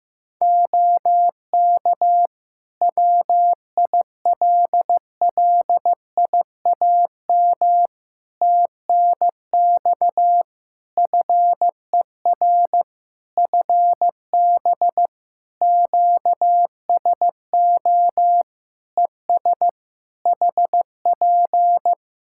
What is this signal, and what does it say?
Morse code, heard as a keyed tone, about 15 wpm, 700 Hz. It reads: OK WILLIAM TNX FER FB QSO ES HP
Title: Practice QSO - Exchange 4 - Ending - 15wpm